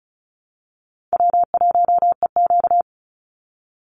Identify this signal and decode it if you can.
Morse code, heard as a keyed tone, about 34 wpm, 700 Hz. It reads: W1EQ
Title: Call Signs Repeated 3x - 35wpm